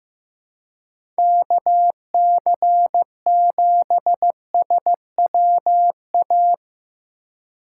Morse code, heard as a keyed tone, about 15 wpm, 700 Hz. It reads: KC7SWA
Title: USA Call Signs - 6 Characters - 3x Character Spacing - 15wpm